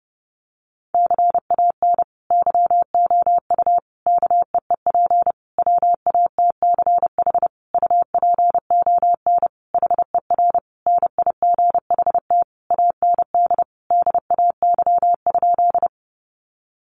Morse code, heard as a keyed tone, about 30 wpm, 700 Hz. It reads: CAN YOU KEEP WATCH UPON HER NIGHT AND DAY?
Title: Sentences from Top 1100 Words 30wpm